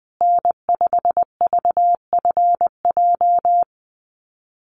Morse code, heard as a keyed tone, about 20 wpm, 700 Hz. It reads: N5VFJ
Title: USA Call Signs Repeated 3x 20wpm